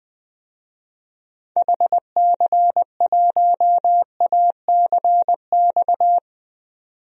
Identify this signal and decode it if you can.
Morse code, heard as a keyed tone, about 21 wpm, 700 Hz. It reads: HC1ACX
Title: DXCC-100 Call Signs - 6 Characters - 20wpm